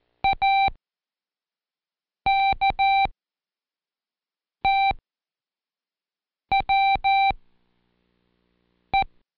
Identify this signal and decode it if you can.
Morse code, heard as a keyed tone, about 13 wpm, 780 Hz. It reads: A K T W E
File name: Code Quick / AudioDownld1 / Seg13-Day2-Practice ATKW